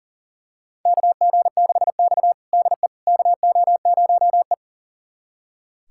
Morse code, heard as a keyed tone, about 40 wpm, 700 Hz. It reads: KG6X DE KO0E
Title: USA Callsign DE Callsign - 4 Characters - 40wpm